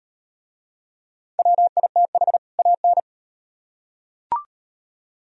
Morse code, heard as a keyed tone, about 35 wpm, 700 Hz. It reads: WITH AN